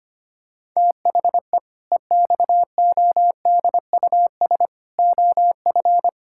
Morse code, heard as a keyed tone, about 25 wpm, 700 Hz. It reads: THE EXODUS OF
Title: Pangram Challenge v2 - 25wpm